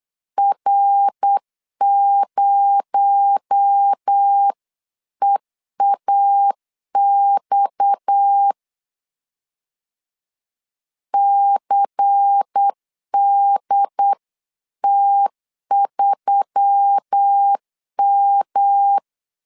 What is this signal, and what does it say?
Morse code, heard as a keyed tone, about 8 wpm, 800 Hz. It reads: R0EAX CDT3M
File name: 8wpm